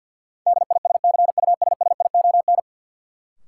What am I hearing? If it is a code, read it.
Morse code, heard as a keyed tone, about 52 wpm, 700 Hz. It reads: DISCUSSION
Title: Sentences from Top 1100 Words - New Word Drill Alt - 50wpm